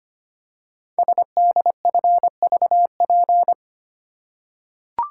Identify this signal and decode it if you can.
Morse code, heard as a keyed tone, about 25 wpm, 700 Hz. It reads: SDFVP
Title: Code Groups - 5 Letter:Numbers - 25wpm